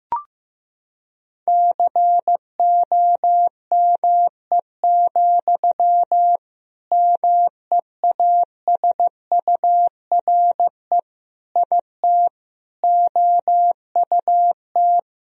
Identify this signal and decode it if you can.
Morse code, heard as a keyed tone, about 15 wpm, 700 Hz. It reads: COME, MEASURE IT OUT
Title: Sentences from Top 1000 Words - Review Part 2 - 15wpm